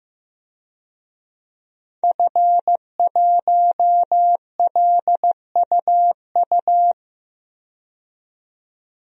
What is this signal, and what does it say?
Morse code, heard as a keyed tone, about 15 wpm, 700 Hz. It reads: F1LUU
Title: DXCC-100 Call Signs - 5 Characters - 15wpm